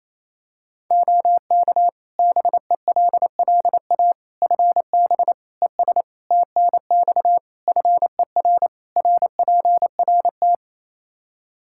Morse code, heard as a keyed tone, about 28 wpm, 700 Hz. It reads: OK BELLA FB ES TNX FER RPRT
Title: K1USN SST QSO Practice 28wpm